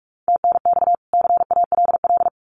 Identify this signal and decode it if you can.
Morse code, heard as a keyed tone, about 45 wpm, 700 Hz. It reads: TNX CALL
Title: Common QSO Phrases 45wpm